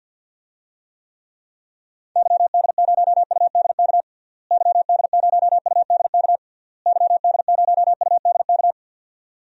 Morse code, heard as a keyed tone, about 52 wpm, 700 Hz. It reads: YD0UDK YD0UDK YD0UDK
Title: DXCC-100 Call Signs - 6 Characters - Repeated 3x - 50wpm